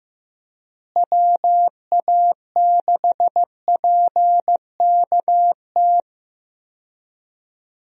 Morse code, heard as a keyed tone, about 15 wpm, 700 Hz. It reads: WA6PKT
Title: USA Call Signs - 6 Characters - Repeated 2x - 15wpm